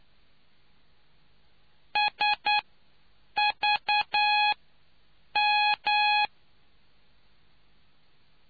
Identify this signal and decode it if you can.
Morse code, heard as a keyed tone, about 9 wpm, spaced as for 7 wpm, 840 Hz. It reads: SVM